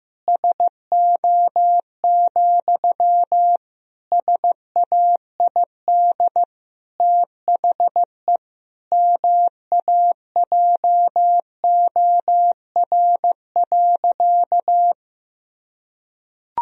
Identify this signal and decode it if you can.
Morse code, heard as a keyed tone, about 15 wpm, 700 Hz. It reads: SO, SAID THE MAJOR.
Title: Sentences from Top 1000 Words - Review Part 4 - 15wpm